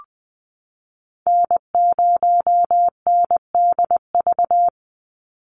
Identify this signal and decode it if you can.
Morse code, heard as a keyed tone, about 20 wpm, 700 Hz. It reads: N0NDV
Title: USA Call Signs 20wpm